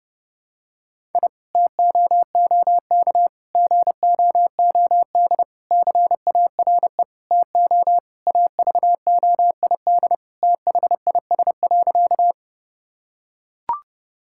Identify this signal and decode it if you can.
Morse code, heard as a keyed tone, about 30 wpm, 700 Hz. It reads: I TOOK GOOD CARE TO AVOID THIS.
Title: Sentences from Top 1000 Words - Review Part 6 - 30wpm